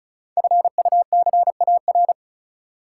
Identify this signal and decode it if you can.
Morse code, heard as a keyed tone, about 36 wpm, 690 Hz. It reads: FUCAR
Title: Code Groups - 5 Letter:Numbers - 35wpm